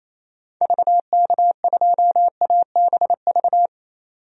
Morse code, heard as a keyed tone, about 27 wpm, 700 Hz. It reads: VK2ABV
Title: DXCC-100 Call Signs - 6 Characters - 28wpm